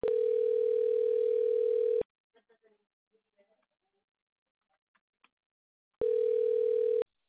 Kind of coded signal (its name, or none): telephone call progress tone